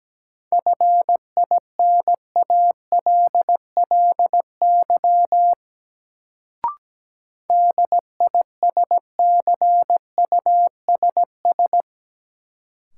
Morse code, heard as a keyed tone, about 17 wpm, 700 Hz. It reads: FINALLY DISCUSS
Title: Sentences from Top 1000 Words - New Word Drill - 17wpm